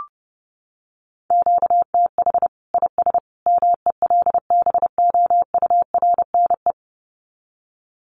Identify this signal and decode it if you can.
Morse code, heard as a keyed tone, about 30 wpm, 700 Hz. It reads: QTH IS MELBOURNE
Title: Common QSO Phrases 30wpm